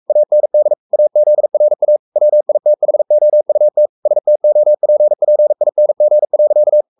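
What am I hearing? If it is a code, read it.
Morse code, heard as a keyed tone, about 44 wpm, 600 Hz. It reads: AND AZRA WITHOUT STOPPING.
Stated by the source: Morse code at 43 wpm sending Ken Follett's novel Never in English, from CW med Gnister.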